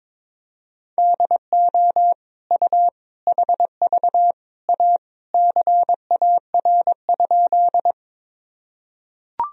Morse code, heard as a keyed tone, about 22 wpm, 700 Hz. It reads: DO U HV A CAR?
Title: Sentences from Top 300 Words and Abbreviations 22wpm